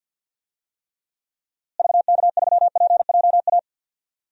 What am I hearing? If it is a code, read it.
Morse code, heard as a keyed tone, about 53 wpm, 700 Hz. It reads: VK3PJA
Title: DXCC-100 Call Signs - 6 Characters - Repeated 2x - 50wpm